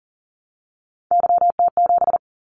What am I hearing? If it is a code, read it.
Morse code, heard as a keyed tone, about 40 wpm, 700 Hz. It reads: YT7
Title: Callsign Prefixes for Top 25 DXCC - 40wpm